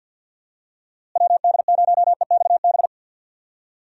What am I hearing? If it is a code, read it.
Morse code, heard as a keyed tone, about 51 wpm, 700 Hz. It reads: WD0EXB
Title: USA Call Signs - 6 Characters - Repeated 2x - 50wpm